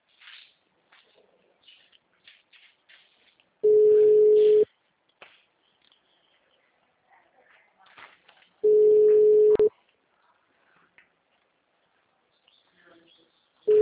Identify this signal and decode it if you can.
Ringback tone (Europe, 425 Hz), with speech.